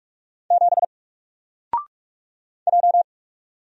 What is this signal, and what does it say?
Morse code, heard as a keyed tone, about 44 wpm, 700 Hz. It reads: 7 J
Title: Single Letter-Number Alt 45wpm